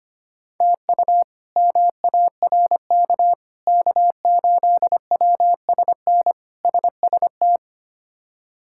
Morse code, heard as a keyed tone, about 25 wpm, 700 Hz. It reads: TU MARK K8WSN SST
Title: K1USN SST QSO Practice - 25wpm